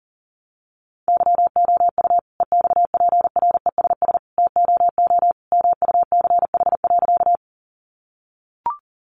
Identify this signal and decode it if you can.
Morse code, heard as a keyed tone, about 40 wpm, 700 Hz. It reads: YOU EXPRESS TOO MUCH.
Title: Sentences from Top 1100 Words 40wpm